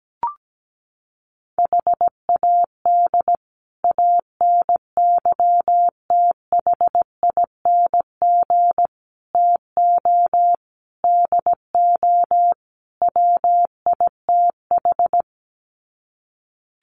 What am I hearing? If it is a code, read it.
Morse code, heard as a keyed tone, about 17 wpm, 700 Hz. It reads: HAD ANYTHING TO DO WITH
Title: Sets of 5 Words Encore 17wpm